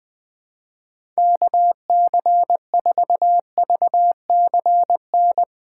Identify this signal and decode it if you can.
Morse code, heard as a keyed tone, about 20 wpm, 700 Hz. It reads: KC4VCN
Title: USA Callsign DE Callsign - 6 Characters - 20wpm